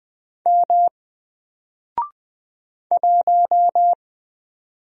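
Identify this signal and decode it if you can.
Morse code, heard as a keyed tone, about 20 wpm, 700 Hz. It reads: M 1
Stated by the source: Single letter and number Morse drill at 20 wpm, from N4LCD.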